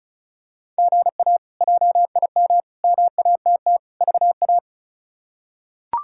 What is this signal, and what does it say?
Morse code, heard as a keyed tone, about 36 wpm, 700 Hz. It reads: GA JIM MATT VA
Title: K1USN SST QSO Practice - 35wpm